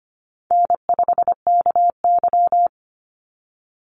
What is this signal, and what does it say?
Morse code, heard as a keyed tone, about 25 wpm, 700 Hz. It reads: N5KY
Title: Call Signs - 25wpm